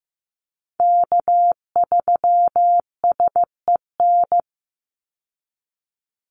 Morse code, heard as a keyed tone, about 15 wpm, 700 Hz. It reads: K3SEN
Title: Call Signs Repeated 2x - 15wpm